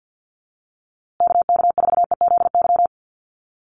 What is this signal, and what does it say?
Morse code, heard as a keyed tone, about 50 wpm, 700 Hz. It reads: KK4EZY
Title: USA Call Signs Repeated 2x 50wpm